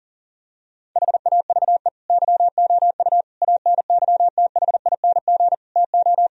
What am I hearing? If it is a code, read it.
Morse code, heard as a keyed tone, about 40 wpm, 700 Hz. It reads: HAVE YOU ANYTHING TO
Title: Sentences from Top 1000 Words - Review Part 1 - 40wpm